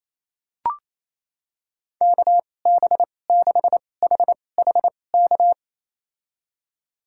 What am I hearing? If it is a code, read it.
Morse code, heard as a keyed tone, about 28 wpm, spaced as for 20 wpm, 700 Hz. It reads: KB6HHK
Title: USA Call Signs - 6 Characters - 2x Character Spacing - 28wpm